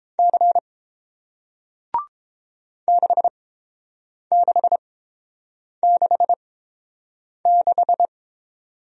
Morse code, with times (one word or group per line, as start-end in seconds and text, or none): 0.2-0.6 C
2.9-8.1 6666